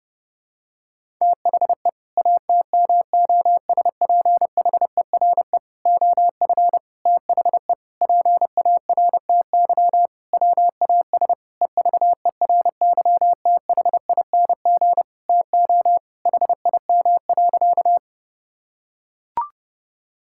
Morse code, heard as a keyed tone, about 32 wpm, 700 Hz. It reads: THE ATMOSPHERE OF THE PARTY WAS EVERYTHING TO HIM.
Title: Sentences from Top 1600 Words 30wpm